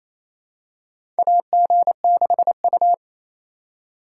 Morse code, heard as a keyed tone, about 28 wpm, 700 Hz. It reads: AG6U
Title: USA Call Signs - 4 Characters - 4x Character Spacing - 28wpm